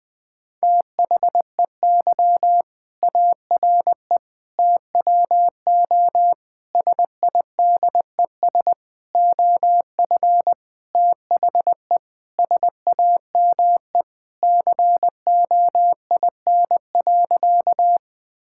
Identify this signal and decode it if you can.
Morse code, heard as a keyed tone, about 20 wpm, 700 Hz. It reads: THEY ARE TWO SIDES OF THE SAME COIN.
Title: Sentences with 7-Word Phrases - Spoken Beginning - 20wpm